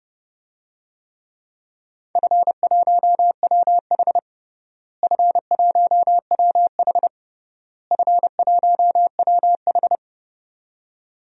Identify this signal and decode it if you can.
Morse code, heard as a keyed tone, about 32 wpm, 700 Hz. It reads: F1WH F1WH F1WH
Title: DXCC-100 Call Signs - 4 Characters - Repeated 3x - 30wpm